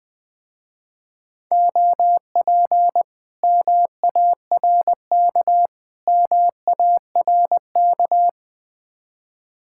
Morse code, heard as a keyed tone, about 20 wpm, 700 Hz. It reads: OP MARK MARK
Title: Speed-Racing Common QSO Phrases Encore - 20wpm